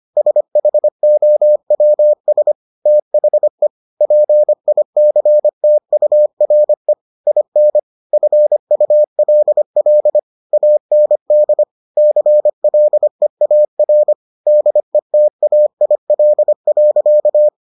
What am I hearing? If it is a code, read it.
Morse code, heard as a keyed tone, about 25 wpm, 600 Hz. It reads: SHOWS THE PICTURE IN FULL AND CLEAR DETAIL.